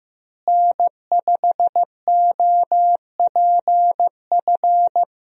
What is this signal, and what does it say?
Morse code, heard as a keyed tone, about 15 wpm, 700 Hz. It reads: N5OPF
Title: USA Call Signs - 5 Characters - Repeated 2x - 15wpm